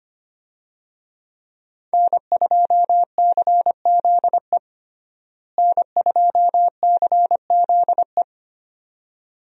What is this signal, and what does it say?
Morse code, heard as a keyed tone, about 25 wpm, 700 Hz. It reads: N2CZE N2CZE
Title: USA Call Signs - 5 Characters - Repeated 2x - 25wpm